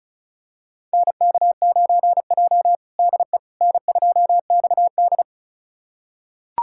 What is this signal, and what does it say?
Morse code, heard as a keyed tone, about 36 wpm, 700 Hz. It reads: NK9J DE N2XD
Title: USA Callsign DE Callsign - 4 Characters - 35wpm